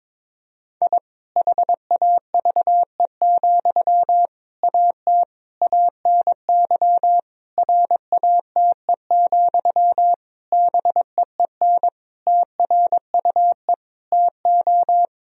Morse code, heard as a keyed tone, about 22 wpm, 700 Hz. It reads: I HAVE, AT ANY RATE, BEEN TRUE TO
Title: Sentences from Top 1000 Words - Review Part 4 - 22wpm